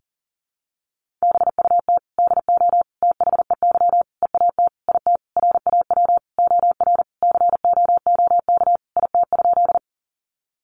Morse code, heard as a keyed tone, about 40 wpm, 700 Hz. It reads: BUT DO THEY EAT IT RAW OR COOK IT?
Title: Sentences from Top 1500 Words 40wpm